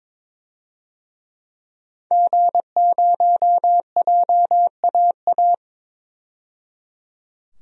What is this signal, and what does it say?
Morse code, heard as a keyed tone, about 22 wpm, 700 Hz. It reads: G0JAA
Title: DXCC-100 Call Signs - 5 Characters - 22wpm